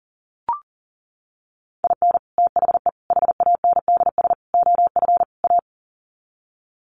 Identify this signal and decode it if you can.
Morse code, heard as a keyed tone, about 40 wpm, 700 Hz. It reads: IN THE HANDS OF A
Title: Sets of 5 Words Encore 40wpm